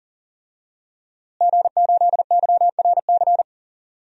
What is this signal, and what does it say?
Morse code, heard as a keyed tone, about 40 wpm, 700 Hz. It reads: G8YRC
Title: DXCC-100 Call Signs - 5 Characters - Repeated 2x - 40wpm